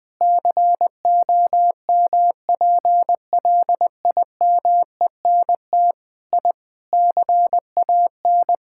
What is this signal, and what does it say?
Morse code, heard as a keyed tone, about 20 wpm, 700 Hz. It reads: COMPLIMENT I CAN
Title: Sentences with 8-Word Phrases - 20wpm